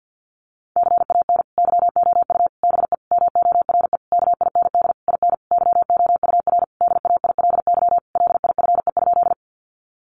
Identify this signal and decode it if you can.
Morse code, heard as a keyed tone, about 50 wpm, 700 Hz. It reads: CAN YOU BE MORE KIND IN YOUR DAILY LIFE?
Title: Sentences from Top 1100 Words 50wpm